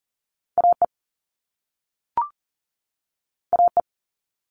Morse code, heard as a keyed tone, about 40 wpm, 700 Hz. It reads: AE AE
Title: All Cut Numbers - 2-Digit 40wpm